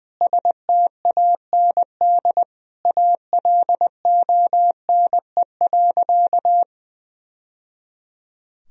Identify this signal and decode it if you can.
Morse code, heard as a keyed tone, about 21 wpm, 700 Hz. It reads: STAND ALONE.
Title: Sentences from Top 1000 Words - Review Part 3 - 20wpm